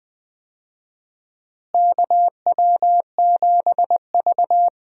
Morse code, heard as a keyed tone, about 20 wpm, 700 Hz. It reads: KW7V
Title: USA Call Signs - 4 Characters - Repeated 2x - 20wpm